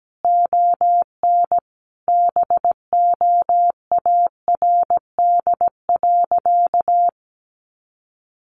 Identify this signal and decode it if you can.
Morse code, heard as a keyed tone, about 17 wpm, 700 Hz. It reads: ON BOARD.
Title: Sentences from Top 700 Words 17wpm